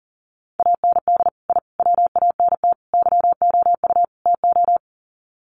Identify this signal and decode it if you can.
Morse code, heard as a keyed tone, about 38 wpm, 700 Hz. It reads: AND I WANT YOU TO
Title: Sets of 5 Words Encore 40wpm